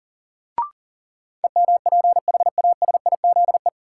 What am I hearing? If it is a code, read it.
Morse code, heard as a keyed tone, about 40 wpm, 700 Hz. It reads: EMPHASIZE
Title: Sentences from Top 1500 Words - New Word Drill - 40wpm